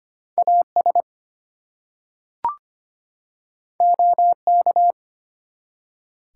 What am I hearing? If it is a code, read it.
Morse code, heard as a keyed tone, about 25 wpm, 700 Hz. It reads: AS OK
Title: Two Letter Words - 25wpm